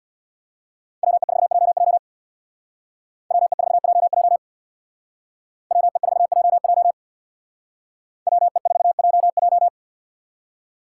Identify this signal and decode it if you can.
Morse code, heard as a keyed tone, about 60 wpm, 700 Hz. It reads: WE4JJ WE4JJ WE4JJ WE4JJ
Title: Speed Racing - USA 2x2 Call Signs 50wpm